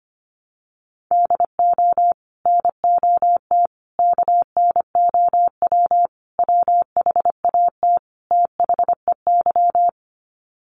Morse code, heard as a keyed tone, about 25 wpm, 700 Hz. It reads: DO NOT KNOW WHAT THEY
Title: Sets of 5 Words 25wpm